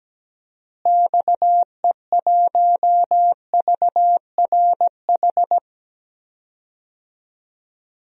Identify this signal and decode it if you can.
Morse code, heard as a keyed tone, about 17 wpm, 700 Hz. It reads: XE1VRH
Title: DXCC-100 Call Signs - 6 Characters - Repeated 3x - 17wpm